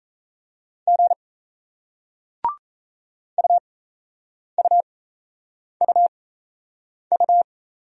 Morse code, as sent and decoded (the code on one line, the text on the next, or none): --. / ..- ..- ..- ..-
G UUUU